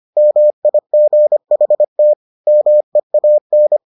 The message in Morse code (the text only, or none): MIGHT MEAN